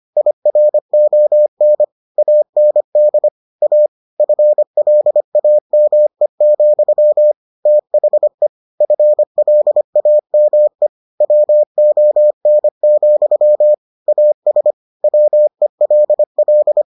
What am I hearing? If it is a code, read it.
Morse code, heard as a keyed tone, about 25 wpm, 600 Hz. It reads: IRON AND A FLAME, THE FLAME WON, AS WELL